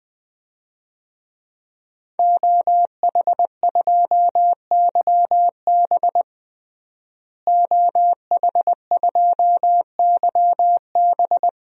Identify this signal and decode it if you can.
Morse code, heard as a keyed tone, about 20 wpm, 700 Hz. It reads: OH2YB OH2YB
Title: DXCC-100 Call Signs - 5 Characters - Repeated 2x - 20wpm